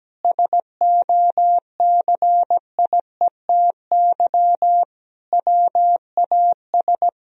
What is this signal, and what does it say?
Morse code, heard as a keyed tone, about 17 wpm, 700 Hz. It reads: SOCIETY WAS
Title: Sentences from Top 1000 Words - Review Part 4 - 17wpm